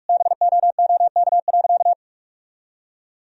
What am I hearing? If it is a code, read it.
Morse code, heard as a keyed tone, about 46 wpm, 700 Hz. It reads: BOOK.
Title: Sentences from Top 1000 Words - Review Part 1 - 45wpm